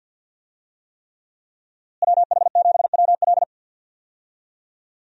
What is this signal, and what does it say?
Morse code, heard as a keyed tone, about 51 wpm, 700 Hz. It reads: WH7WL
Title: USA Call Signs - 5 Characters - Repeated 3x - 50wpm